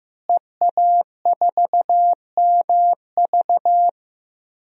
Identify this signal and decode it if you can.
Morse code, heard as a keyed tone, about 15 wpm, 700 Hz. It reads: EA4MV
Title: DXCC-100 Call Signs - 5 Characters - Repeated 3x - 15wpm